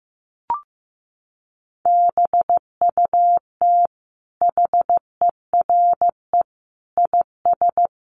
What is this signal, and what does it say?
Morse code, heard as a keyed tone, about 15 wpm, 700 Hz. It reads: BUT HERE IS
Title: Sentences from Top 1500 Words 15wpm